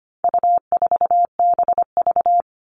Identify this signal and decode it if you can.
Morse code, heard as a keyed tone, about 25 wpm, 700 Hz. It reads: U4BV
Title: All Cut Numbers - 4-Digit 25wpm